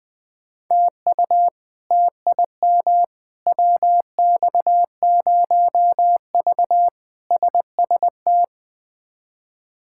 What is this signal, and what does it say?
Morse code, heard as a keyed tone, about 20 wpm, 700 Hz. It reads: TU TIM WX0V SST
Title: K1USN SST QSO Practice - 20wpm